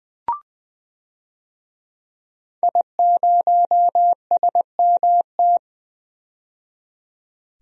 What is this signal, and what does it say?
Morse code, heard as a keyed tone, about 20 wpm, 700 Hz. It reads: I0SMT
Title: DXCC-100 Call Signs - 5 Characters - 20wpm